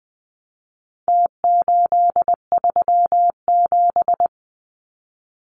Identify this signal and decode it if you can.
Morse code, heard as a keyed tone, about 20 wpm, 700 Hz. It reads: T837
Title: Numbers - Contest Style - 4-Digit 20wpm